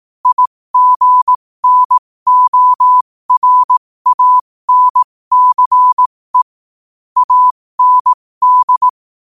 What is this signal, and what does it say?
Morse code, heard as a keyed tone, about 18 wpm, spaced as for 15 wpm, 1000 Hz. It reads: IGNORANCE AND
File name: qotdfiles / 15wpm